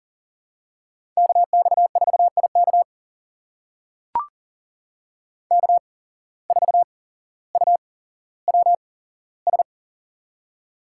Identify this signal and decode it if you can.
Morse code, heard as a keyed tone, about 40 wpm, spaced as for 17 wpm, 700 Hz. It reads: KX4IK K4UWS
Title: USA Call Signs - 5 Characters - 8x Character Spacing - 40wpm